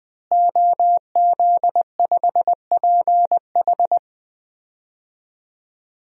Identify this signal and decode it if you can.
Morse code, heard as a keyed tone, about 20 wpm, 700 Hz. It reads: OZ5PH
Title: DXCC-100 Call Signs - 5 Characters - Repeated 3x - 20wpm